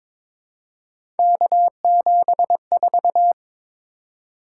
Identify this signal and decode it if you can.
Morse code, heard as a keyed tone, about 22 wpm, 700 Hz. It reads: K74
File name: practice / 22wpm / Three Letter-Number Combinations - 22wpm